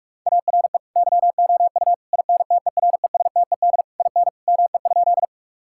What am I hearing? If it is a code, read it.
Morse code, heard as a keyed tone, about 45 wpm, 700 Hz. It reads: ARE YOU INTERESTED IN ME?
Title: Sentences from Top 1000 Words - Review Part 6 - 45wpm